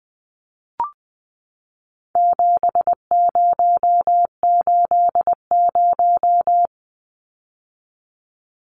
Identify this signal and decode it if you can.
Morse code, heard as a keyed tone, about 20 wpm, 700 Hz. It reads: 7080
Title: Numbers - 4-Digit 20wpm